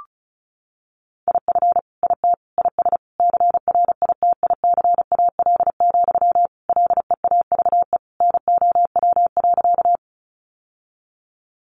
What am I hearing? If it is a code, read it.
Morse code, heard as a keyed tone, about 34 wpm, 700 Hz. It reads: IF IT IS CRITICAL, LEAVE NOW.
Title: Sentences from Top 1100 Words 35wpm